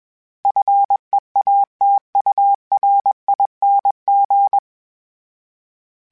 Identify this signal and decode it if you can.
Morse code, heard as a keyed tone, about 21 wpm, 800 Hz. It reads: FEATURING